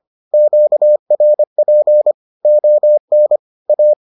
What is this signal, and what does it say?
Morse code, heard as a keyed tone, about 25 wpm, 600 Hz. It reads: QRP ON A